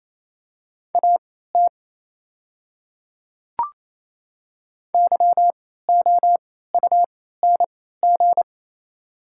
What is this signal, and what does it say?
Morse code, heard as a keyed tone, about 28 wpm, spaced as for 16 wpm, 690 Hz. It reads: AT YOUNG
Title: 3x Character Spacing - Top 500 Words - 28wpm